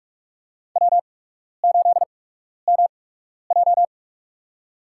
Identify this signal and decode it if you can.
Morse code, heard as a keyed tone, about 46 wpm, spaced as for 12 wpm, 700 Hz. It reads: W8MJ
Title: USA Call Signs - 4 Characters - 8x Character Spacing - 45wpm